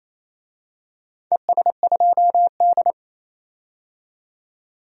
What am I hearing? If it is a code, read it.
Morse code, heard as a keyed tone, about 28 wpm, 700 Hz. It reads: ES2D
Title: DXCC-100 Call Signs - 4 Characters - 28wpm